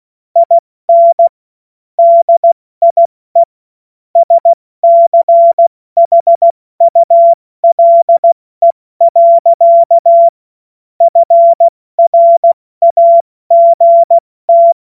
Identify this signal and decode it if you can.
Morse code, heard as a keyed tone, about 16 wpm, 680 Hz. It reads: IN DIE SCHULE. FRAGT
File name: Morsekurs / Witze / Witz 24. August 2021, 70 F80